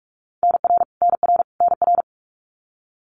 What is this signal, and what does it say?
Morse code, heard as a keyed tone, about 44 wpm, 700 Hz. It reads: NR NR NR
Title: Abbreviations Repeated 3x - 45wpm